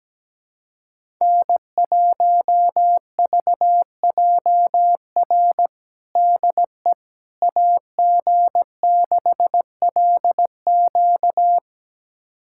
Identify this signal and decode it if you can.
Morse code, heard as a keyed tone, about 17 wpm, 700 Hz. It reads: N1VJR DE AG6LQ
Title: USA Callsign DE Callsign - 5 Characters - 17wpm